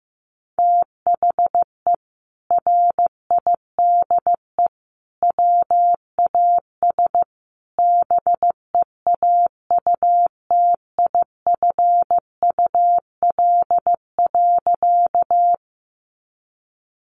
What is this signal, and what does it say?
Morse code, heard as a keyed tone, about 15 wpm, 700 Hz. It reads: THE RIDE WAS BEAUTIFUL.
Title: Sentences from Top 1200 Words 15wpm